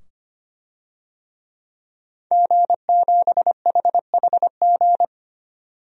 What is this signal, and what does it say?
Morse code, heard as a keyed tone, about 25 wpm, 700 Hz. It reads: G7HHG
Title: DXCC-100 Call Signs - 5 Characters - 25wpm